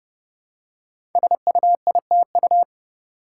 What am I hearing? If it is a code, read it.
Morse code, heard as a keyed tone, about 30 wpm, 700 Hz. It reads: SUITU
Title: Code Groups - 5 Letters - 30wpm